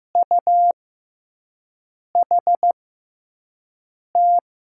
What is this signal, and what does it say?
Morse code, heard as a keyed tone, about 15 wpm, spaced as for 5 wpm, 700 Hz. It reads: UHT
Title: USA Call Signs - 6 Characters - 6x Character Spacing - 15wpm